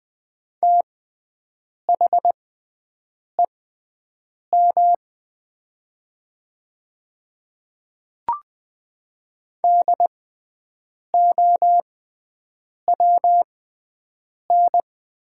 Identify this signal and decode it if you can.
Morse code, heard as a keyed tone, about 20 wpm, spaced as for 7 wpm, 700 Hz. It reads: THEM DOWN
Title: 6x Character Spacing - Top 100 Words - 20wpm